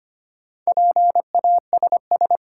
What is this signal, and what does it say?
Morse code, heard as a keyed tone, about 25 wpm, 700 Hz. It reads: PASS
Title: Sentences from Top 400 Words - New Word Drill - 25wpm